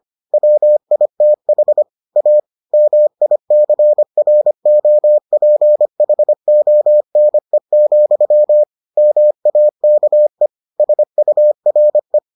Morse code, heard as a keyed tone, about 25 wpm, 600 Hz. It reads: WITH A MICROPHONE, MAKE SURE